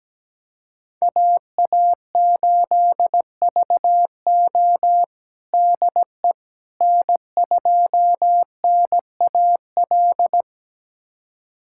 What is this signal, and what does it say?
Morse code, heard as a keyed tone, about 17 wpm, 700 Hz. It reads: AA8VO DE N2NAL
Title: USA Callsign DE Callsign - 5 Characters - 17wpm